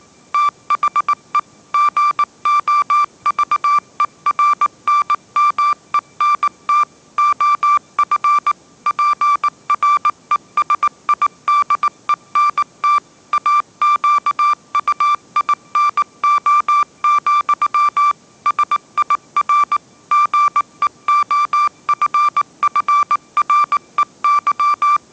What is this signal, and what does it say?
Morse code, heard as a keyed tone, about 19 wpm, 1180 Hz. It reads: THE GOVERNMENT OF PRESIDENT AQUINO, SIR GEOFFREY
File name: mrx083 20wpm